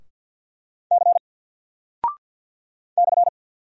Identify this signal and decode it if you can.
Morse code, heard as a keyed tone, about 48 wpm, 700 Hz. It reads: C /
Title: Single Character (Letter:Number:Punctuation:Prosign) Alt 50wpm